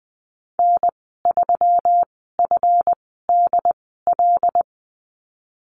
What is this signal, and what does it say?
Morse code, heard as a keyed tone, about 20 wpm, spaced as for 15 wpm, 700 Hz. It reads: N3FDL
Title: 186 Lesson 30g - Letter b - Call Signs 20wpm